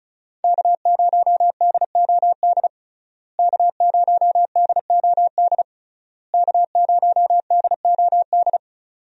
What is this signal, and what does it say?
Morse code, heard as a keyed tone, about 34 wpm, 700 Hz. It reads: K0DOD K0DOD K0DOD
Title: USA Call Signs - 5 Characters - Repeated 3x - 35wpm